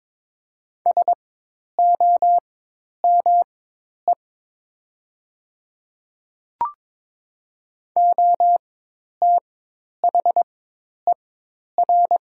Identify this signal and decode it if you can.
Morse code, heard as a keyed tone, about 22 wpm, spaced as for 10 wpm, 700 Hz. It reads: SOME OTHER
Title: 4x Character Spacing - Top 100 Words - 22wpm